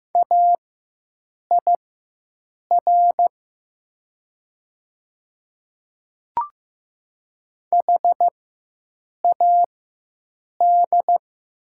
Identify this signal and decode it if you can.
Morse code, heard as a keyed tone, about 15 wpm, spaced as for 7 wpm, 700 Hz. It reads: AIR HAD
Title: 4x Character Spacing - Top 200 Words - 15wpm